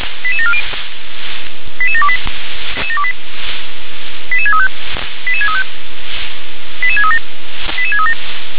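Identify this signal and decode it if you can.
Five-tone selective call (ZVEI), heard as quick runs of five tones, tones of about 70 ms. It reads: 88740 88728 88635 88635 88637 88637
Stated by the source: ZVEI five-tone selective call received by SDR.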